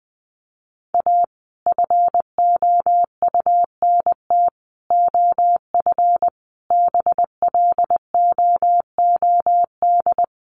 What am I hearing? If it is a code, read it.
Morse code, heard as a keyed tone, about 20 wpm, 700 Hz. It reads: A FOUNT OF BLOOD
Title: A Sound Of Thunder 20wpm